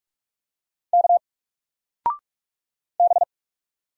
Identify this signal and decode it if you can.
Morse code, heard as a keyed tone, about 46 wpm, 700 Hz. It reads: K B
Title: Single Character (Letter:Number:Punctuation:Prosign) 45wpm